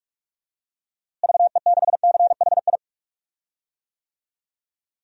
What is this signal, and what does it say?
Morse code, heard as a keyed tone, about 46 wpm, 700 Hz. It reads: VE6CHI
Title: DXCC-100 Call Signs - 6 Characters - Repeated 2x - 45wpm